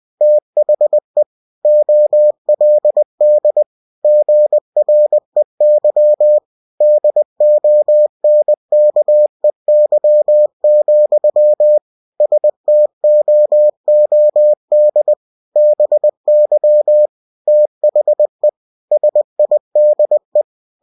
Morse code, heard as a keyed tone, about 20 wpm, 600 Hz. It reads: THE OLD GREY DONKEY, STOOD BY THE SIDE
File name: cw-books / winnie-the-pooh / winnie-20wpm-page-0044